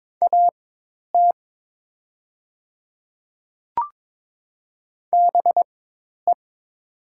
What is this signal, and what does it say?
Morse code, heard as a keyed tone, about 22 wpm, spaced as for 10 wpm, 700 Hz. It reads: AT BE
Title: 4x Character Spacing - Top 200 Words - 22wpm